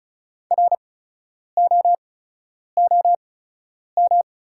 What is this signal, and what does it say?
Morse code, heard as a keyed tone, about 36 wpm, spaced as for 10 wpm, 690 Hz. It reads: ROOM
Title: 8x Character Spacing - Top 500 Words - 35wpm